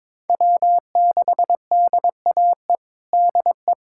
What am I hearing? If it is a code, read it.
Morse code, heard as a keyed tone, about 22 wpm, 700 Hz. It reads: W6DAE DE
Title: USA Callsign DE Callsign - 5 Characters - 22wpm